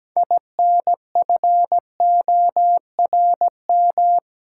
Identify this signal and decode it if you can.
Morse code, heard as a keyed tone, about 17 wpm, 700 Hz. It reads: INFORM
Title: Six Letter Words - 17wpm